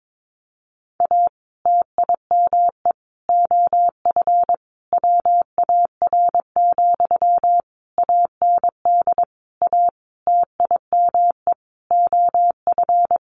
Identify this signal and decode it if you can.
Morse code, heard as a keyed tone, about 22 wpm, 700 Hz. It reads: A TIME OF WAR, AND A TIME OF
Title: Sentences from Top 1500 Words 22wpm